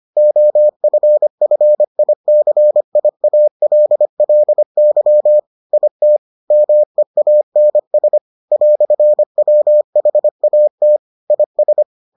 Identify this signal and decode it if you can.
Morse code, heard as a keyed tone, about 25 wpm, 600 Hz. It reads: OFFICIALLY IT MEANS "WHAT IS